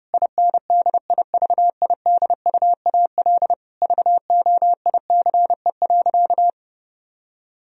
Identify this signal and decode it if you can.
Morse code, heard as a keyed tone, about 30 wpm, 700 Hz. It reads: INDIVIDUAL VOICE.
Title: Sentences from Top 1000 Words - Review Part 4 - 30wpm